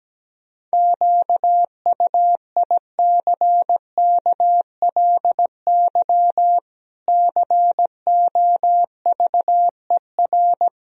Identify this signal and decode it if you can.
Morse code, heard as a keyed tone, about 17 wpm, 700 Hz. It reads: QUICKLY COVER
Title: Sentences from Top 1000 Words - Review Part 1 - 17wpm